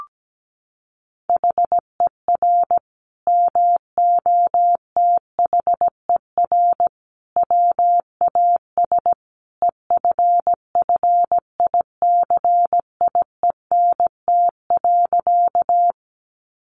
Morse code, heard as a keyed tone, about 17 wpm, 700 Hz. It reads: HER MOTHER WAS EFFICIENT.
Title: Sentences from Top 1300 Words 17wpm